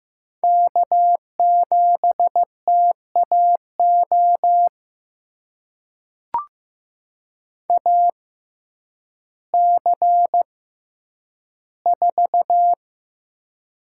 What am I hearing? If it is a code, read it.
Morse code, heard as a keyed tone, about 15 wpm, spaced as for 9 wpm, 700 Hz. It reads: K7TAO AC4
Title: USA Call Signs - 5 Characters - 6x Character Spacing - 15wpm